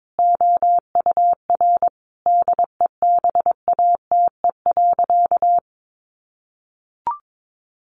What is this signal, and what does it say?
Morse code, heard as a keyed tone, about 22 wpm, 700 Hz. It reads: OUR DEBATE.
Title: Sentences from Top 1500 Words 22wpm